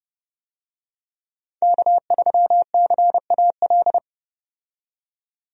Morse code, heard as a keyed tone, about 30 wpm, 700 Hz. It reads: K3CAL
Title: USA Call Signs - 5 Characters - 30wpm